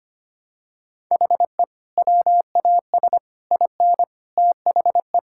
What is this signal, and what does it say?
Morse code, heard as a keyed tone, about 25 wpm, 700 Hz. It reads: HE WAS IN THE
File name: practice / 25wpm / Sentences with 9-Word Phrases - 25wpm